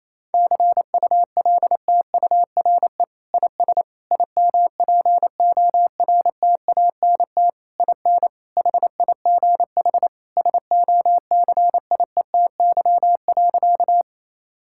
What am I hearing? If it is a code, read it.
Morse code, heard as a keyed tone, about 28 wpm, 700 Hz. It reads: CULTURE IS IMPORTANT IN HIGH SOCIETY.
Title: Sentences from Top 1000 Words - Review Part 6 - 28wpm